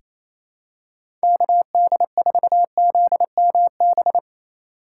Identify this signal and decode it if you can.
Morse code, heard as a keyed tone, about 28 wpm, 700 Hz. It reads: KD4ZMB
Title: USA Call Signs - 6 Characters - 3x Character Spacing - 28wpm